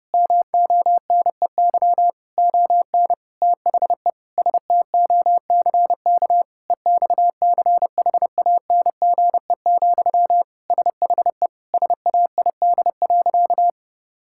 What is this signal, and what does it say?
Morse code, heard as a keyed tone, about 30 wpm, 700 Hz. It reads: MONEY ON THE STOCK EXCHANGE, SHE SAID.
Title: Sentences from Top 1000 Words - Review Part 6 - 30wpm